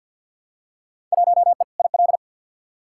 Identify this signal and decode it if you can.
Morse code, heard as a keyed tone, about 47 wpm, 700 Hz. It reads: 1E IL
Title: Field Day Exchanges - 50wpm